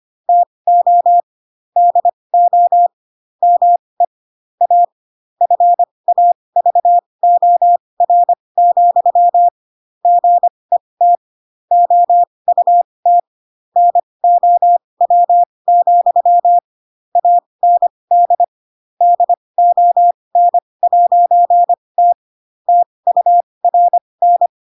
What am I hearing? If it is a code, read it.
Morse code, heard as a keyed tone, about 25 wpm, spaced as for 20 wpm, 700 Hz. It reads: TO DO ME A FAVOR, GET OUT NOW, AND DON'T TURN
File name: Pinocchio / Pinocchio ch4 15 25 ik1hge